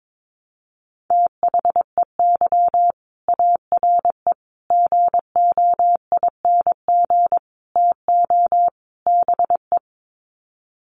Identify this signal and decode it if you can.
Morse code, heard as a keyed tone, about 22 wpm, 700 Hz. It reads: THEY ARE GOING TO BE
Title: Sets of 5 Words Encore2 22wpm